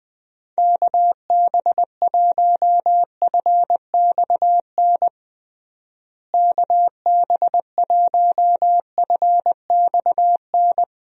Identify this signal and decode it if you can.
Morse code, heard as a keyed tone, about 21 wpm, 700 Hz. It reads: KB1FXN KB1FXN
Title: USA Call Signs - 6 Characters - Repeated 3x - 20wpm